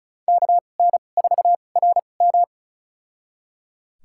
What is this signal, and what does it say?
Morse code, heard as a keyed tone, about 34 wpm, spaced as for 25 wpm, 700 Hz. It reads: KN4RM
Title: USA Call Signs - 5 Characters - 2x Character Spacing - 35wpm